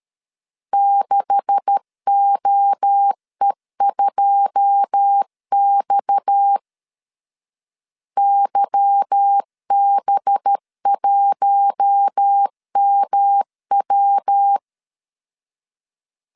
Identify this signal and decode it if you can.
Morse code, heard as a keyed tone, about 13 wpm, 800 Hz. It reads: 6OE2X YB1MW
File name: CW / DET / 1 9 1500C S C 12wpm 3811